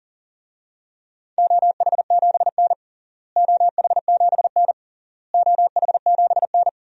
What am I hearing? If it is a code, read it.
Morse code, heard as a keyed tone, about 40 wpm, 700 Hz. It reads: OH7N OH7N OH7N
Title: DXCC-100 Call Signs - 4 Characters - Repeated 3x - 40wpm